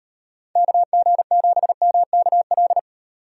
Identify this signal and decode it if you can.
Morse code, heard as a keyed tone, about 38 wpm, 700 Hz. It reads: KG7MKL